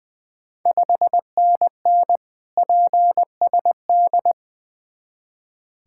Morse code, heard as a keyed tone, about 20 wpm, 690 Hz. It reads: 5NN PSD
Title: Ontario QSO Party Exchange 20wpm